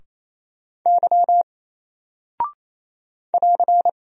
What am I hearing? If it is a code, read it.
Morse code, heard as a keyed tone, about 28 wpm, 700 Hz. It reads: Y <AR>
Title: Single Character (Letter:Number:Punctuation:Prosign) Alt 28wpm